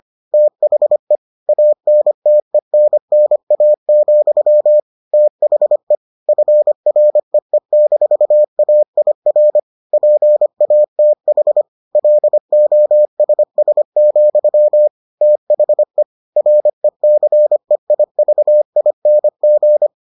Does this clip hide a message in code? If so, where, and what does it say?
Yes: Morse code from 0.3 to 19.9 s, heard as a keyed tone, about 25 wpm, 600 Hz. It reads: THE ANTENNA, THE FREE-AIR PATH LOSS, THE RECEIVING